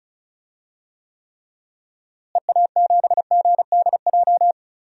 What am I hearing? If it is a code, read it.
Morse code, heard as a keyed tone, about 36 wpm, 700 Hz. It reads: EA7GDJ
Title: DXCC-100 Call Signs - 6 Characters - Repeated 3x - 35wpm